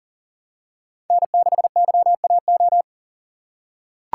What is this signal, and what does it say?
Morse code, heard as a keyed tone, about 40 wpm, 700 Hz. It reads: N6YAO
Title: USA Call Signs - 5 Characters - 2x Character Spacing - 40wpm